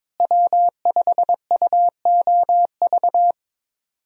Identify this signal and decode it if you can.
Morse code, heard as a keyed tone, about 22 wpm, 700 Hz. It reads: W5UOV
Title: USA Call Signs - 5 Characters - Repeated 2x - 22wpm